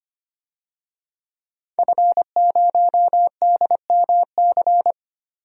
Morse code, heard as a keyed tone, about 25 wpm, 700 Hz. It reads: F0DMC
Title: DXCC-100 Call Signs - 5 Characters - 25wpm